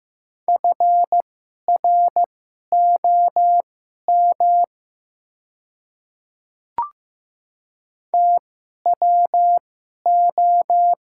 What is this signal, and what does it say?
Morse code, heard as a keyed tone, about 15 wpm, spaced as for 11 wpm, 700 Hz. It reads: FROM TWO
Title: 2x Character Spacing - Top 100 Words - 15wpm